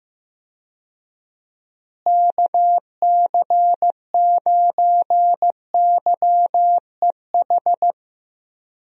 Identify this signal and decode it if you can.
Morse code, heard as a keyed tone, about 15 wpm, 700 Hz. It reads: KC9YEH
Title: USA Call Signs - 6 Characters - Repeated 2x - 15wpm